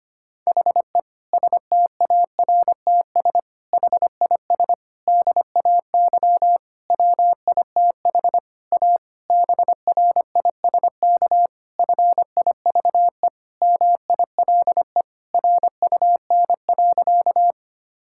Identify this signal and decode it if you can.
Morse code, heard as a keyed tone, about 25 wpm, 700 Hz. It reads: HE STARTS HIS DAY WITH A BRISK FIVE MILE RUN.
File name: practice / 25wpm / Difficult Sentences with Hints - 25wpm